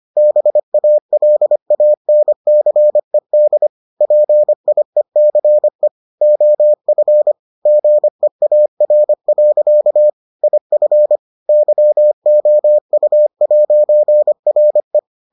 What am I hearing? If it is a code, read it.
Morse code, heard as a keyed tone, about 25 wpm, 600 Hz. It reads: BALANCED PIECE OF GEAR. IF YOU'RE